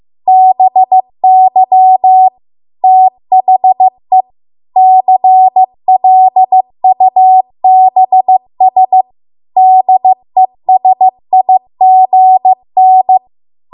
Morse code, heard as a keyed tone, about 15 wpm, 750 Hz. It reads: BY THE CLUBS DESIGN